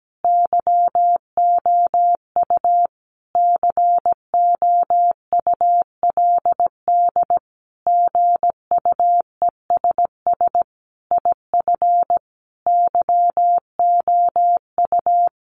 Morse code, heard as a keyed tone, about 17 wpm, 700 Hz. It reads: YOU COULD GUESS IF YOU
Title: Sentences from Top 1500 Words 17wpm